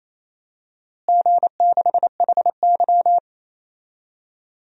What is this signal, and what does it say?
Morse code, heard as a keyed tone, about 28 wpm, 700 Hz. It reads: G6HY
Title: DXCC-100 Call Signs - 4 Characters - Repeated 2x - 28wpm